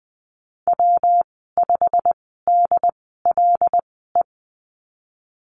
Morse code, heard as a keyed tone, about 20 wpm, spaced as for 15 wpm, 700 Hz. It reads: W5DLE
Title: 122 Lesson 20g - Number 3 - Call Signs 20wpm